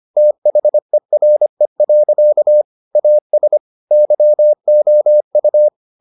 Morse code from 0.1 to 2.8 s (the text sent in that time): THERE.